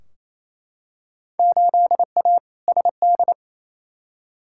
Morse code, heard as a keyed tone, about 28 wpm, 700 Hz. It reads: 8A SD
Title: Field Day Exchanges Encore - 28wpm